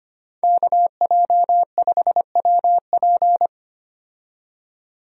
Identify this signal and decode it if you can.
Morse code, heard as a keyed tone, about 25 wpm, 690 Hz. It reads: KJ5WP
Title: USA Call Signs - 5 Characters - 25wpm